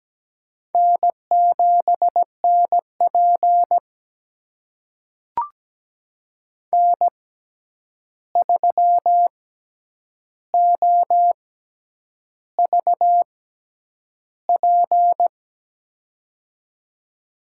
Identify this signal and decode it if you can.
Morse code, heard as a keyed tone, about 17 wpm, spaced as for 8 wpm, 700 Hz. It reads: N7NP N3OVP
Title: USA Call Signs - 6x Character Spacing - 17wpm